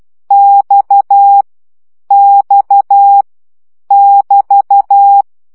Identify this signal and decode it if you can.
Morse code, heard as a keyed tone, about 12 wpm, 800 Hz. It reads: X X <BT>
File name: CW gyakorlas x 10 baud